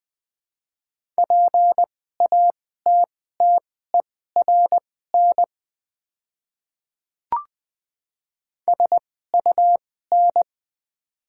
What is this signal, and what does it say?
Morse code, heard as a keyed tone, about 20 wpm, spaced as for 14 wpm, 700 Hz. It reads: PATTERN SUN
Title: 2x Character Spacing - Top 500 Words - 20wpm